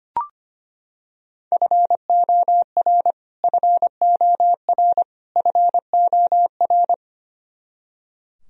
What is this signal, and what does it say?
Morse code, heard as a keyed tone, about 25 wpm, 700 Hz. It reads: FOR FOR FOR
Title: Top 100 Words Repeated 25wpm